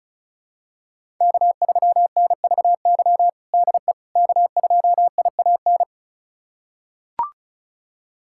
Morse code, heard as a keyed tone, about 34 wpm, 700 Hz. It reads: K3NVY DE K2IAN
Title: USA Callsign DE Callsign - 5 Characters - 35wpm